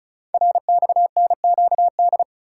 Morse code, heard as a keyed tone, about 36 wpm, 700 Hz. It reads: RXNQD
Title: Code Groups - 5 Letters - 35wpm